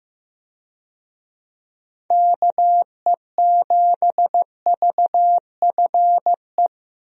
Morse code, heard as a keyed tone, about 15 wpm, 700 Hz. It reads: KE7VFE
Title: USA Call Signs - 6 Characters - Repeated 3x - 15wpm